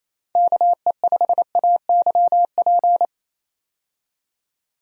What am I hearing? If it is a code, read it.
Morse code, heard as a keyed tone, about 28 wpm, 700 Hz. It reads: KE5AYP
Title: USA Call Signs - 6 Characters - 28wpm